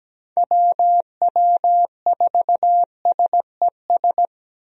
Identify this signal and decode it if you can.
Morse code, heard as a keyed tone, about 17 wpm, 700 Hz. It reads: WW4SES
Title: USA Call Signs - 6 Characters - Repeated 2x - 17wpm